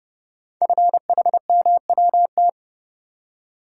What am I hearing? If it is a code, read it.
Morse code, heard as a keyed tone, about 30 wpm, 700 Hz. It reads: FHMWT
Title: Code Groups - 5 Letters - 30wpm